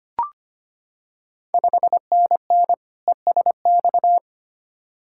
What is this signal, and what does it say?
Morse code, heard as a keyed tone, about 25 wpm, 700 Hz. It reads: 5NN ESX
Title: Ontario QSO Party Exchange 25wpm